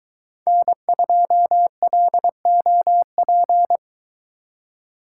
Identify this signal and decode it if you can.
Morse code, heard as a keyed tone, about 22 wpm, 700 Hz. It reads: N2LOP